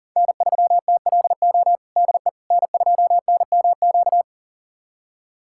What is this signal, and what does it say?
Morse code, heard as a keyed tone, about 40 wpm, 700 Hz. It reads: N3TLO DE N2NMQ
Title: USA Callsign DE Callsign - 5 Characters - 40wpm